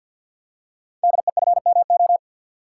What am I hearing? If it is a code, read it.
Morse code, heard as a keyed tone, about 52 wpm, 700 Hz. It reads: DEFMO
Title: Code Groups - 5 Letters - 50wpm